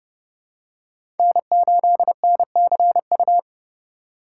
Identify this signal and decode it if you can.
Morse code, heard as a keyed tone, about 30 wpm, 700 Hz. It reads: N8NCU
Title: USA Call Signs - 8x Character Spacing - 30wpm